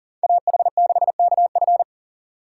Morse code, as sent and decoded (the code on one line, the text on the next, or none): .- .... -.... -.- ..-.
AH6KF